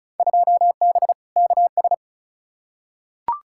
Morse code, heard as a keyed tone, about 34 wpm, 690 Hz. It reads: 2B KS
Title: Field Day Exchanges Encore - 35wpm